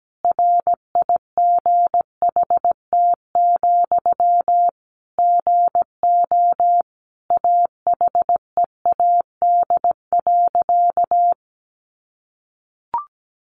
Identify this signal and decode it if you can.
Morse code, heard as a keyed tone, about 17 wpm, 700 Hz. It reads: RIGHT, GO AHEAD.
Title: Sentences from Top 1100 Words 17wpm